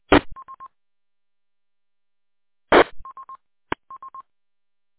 There is no code.